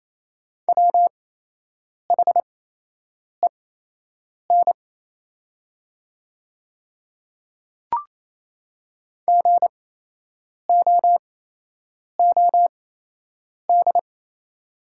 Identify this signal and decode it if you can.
Morse code, heard as a keyed tone, about 27 wpm, spaced as for 8 wpm, 700 Hz. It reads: WHEN GOOD
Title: 8x Character Spacing - Top 200 Words - 28wpm